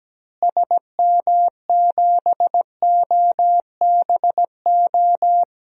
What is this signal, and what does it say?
Morse code, heard as a keyed tone, about 17 wpm, 700 Hz. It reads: SM7OBO
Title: DXCC-100 Call Signs - 6 Characters - Repeated 3x - 17wpm